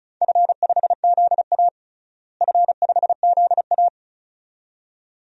Morse code, heard as a keyed tone, about 36 wpm, 700 Hz. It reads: F5ZA F5ZA
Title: DXCC-100 Call Signs - 4 Characters - Repeated 2x - 35wpm